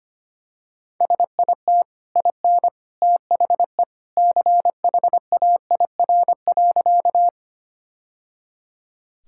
Morse code, heard as a keyed tone, about 25 wpm, 700 Hz. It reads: SIT IN THE CHAIR.
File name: Easy Sentences with Hints - 25wpm